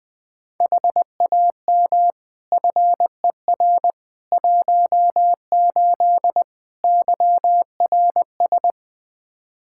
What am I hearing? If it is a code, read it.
Morse code, heard as a keyed tone, about 20 wpm, 700 Hz. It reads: HAM FER 18 YRS
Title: Practice QSO - Exchange 3 - More Info - 20wpm